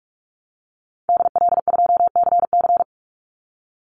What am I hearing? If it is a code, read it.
Morse code, heard as a keyed tone, about 44 wpm, 700 Hz. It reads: DL2CC
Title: Call Signs Repeated 2x - 45wpm